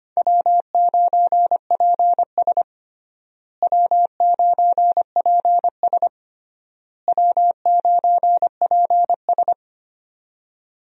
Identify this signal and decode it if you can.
Morse code, heard as a keyed tone, about 25 wpm, 700 Hz. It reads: W9PS W9PS W9PS
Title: USA Call Signs - 4 Characters - Repeated 3x - 25wpm